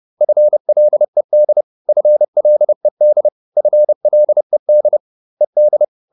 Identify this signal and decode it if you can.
Morse code, heard as a keyed tone, about 30 wpm, 600 Hz. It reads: FLED FLED FLED ED